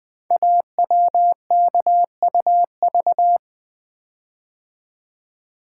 Morse code, heard as a keyed tone, about 20 wpm, 700 Hz. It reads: AWKUV
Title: Code Groups - 5 Letters - 20wpm